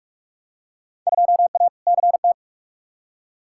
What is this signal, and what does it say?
Morse code, heard as a keyed tone, about 46 wpm, 700 Hz. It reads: 2A CT
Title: Field Day Exchanges Encore - 45wpm